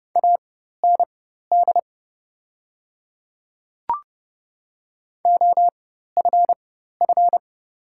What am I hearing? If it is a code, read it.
Morse code, heard as a keyed tone, about 30 wpm, spaced as for 14 wpm, 700 Hz. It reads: AND OFF
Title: 4x Character Spacing - Top 200 Words - 30wpm